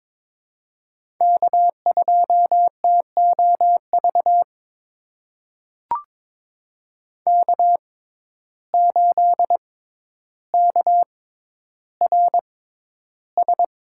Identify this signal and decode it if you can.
Morse code, heard as a keyed tone, about 21 wpm, spaced as for 11 wpm, 700 Hz. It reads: K2TOV K8KRS
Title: USA Call Signs - 5 Characters - 6x Character Spacing - 22wpm